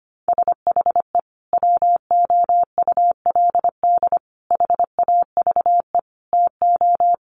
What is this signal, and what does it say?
Morse code, heard as a keyed tone, about 25 wpm, 700 Hz. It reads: SHE WOULD HAVE TO
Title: Sentences from Top 1200 Words 25wpm